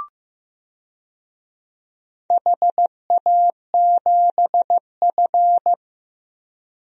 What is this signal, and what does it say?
Morse code, heard as a keyed tone, about 15 wpm, 700 Hz. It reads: HA7F
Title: DXCC-100 Call Signs - 4 Characters - 15wpm